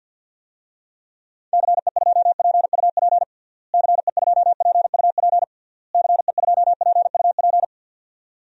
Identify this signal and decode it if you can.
Morse code, heard as a keyed tone, about 52 wpm, 700 Hz. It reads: CE2PUP CE2PUP CE2PUP
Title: DXCC-100 Call Signs - 6 Characters - Repeated 3x - 50wpm